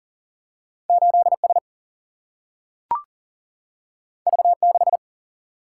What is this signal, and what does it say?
Morse code, heard as a keyed tone, about 40 wpm, 700 Hz. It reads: 8 S V 6
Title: Single-Letter Single-Number 40wpm